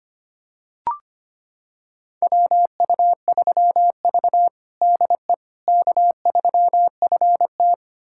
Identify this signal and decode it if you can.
Morse code, heard as a keyed tone, about 25 wpm, 700 Hz. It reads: WU3V DE K3FT
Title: USA Callsign DE Callsign - 4 Characters - 25wpm